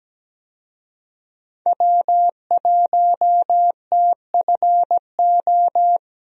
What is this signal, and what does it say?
Morse code, heard as a keyed tone, about 17 wpm, 700 Hz. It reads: W1TFO
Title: USA Call Signs - 5 Characters - Repeated 2x - 17wpm